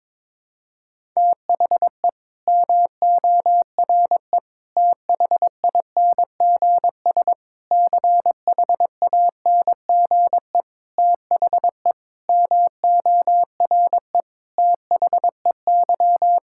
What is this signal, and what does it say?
Morse code, heard as a keyed tone, about 21 wpm, 700 Hz. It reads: THE MORE THINGS CHANGE THE MORE THEY
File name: practice / 22wpm / Sets of 8 Words - 22wpm